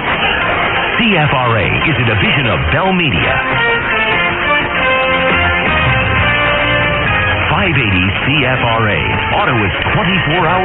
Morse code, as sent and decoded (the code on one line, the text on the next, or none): none